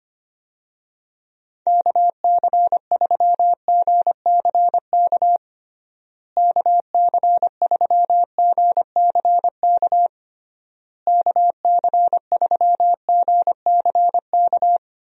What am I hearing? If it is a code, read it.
Morse code, heard as a keyed tone, about 25 wpm, 700 Hz. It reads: KC3GCK KC3GCK KC3GCK
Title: USA Call Signs - 6 Characters - Repeated 3x - 25wpm